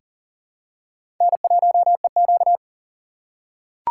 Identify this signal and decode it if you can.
Morse code, heard as a keyed tone, about 40 wpm, 700 Hz. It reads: N1EQ
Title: USA Call Signs - 4 Characters - 6x Character Spacing - 40wpm